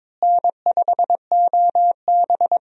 Morse code, heard as a keyed tone, about 22 wpm, 700 Hz. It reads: N5OB
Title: USA Call Signs - 4 Characters - Repeated 2x - 22wpm